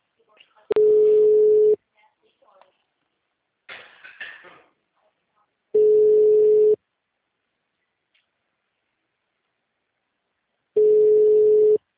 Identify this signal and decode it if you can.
Ringback tone (Europe, 425 Hz), with speech.